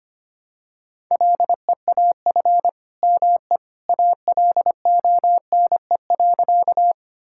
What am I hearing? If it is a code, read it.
Morse code, heard as a keyed tone, about 25 wpm, 700 Hz. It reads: LEAF ME ALONE.
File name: practice / 25wpm / All-Age Appropriate Jokes - 25wpm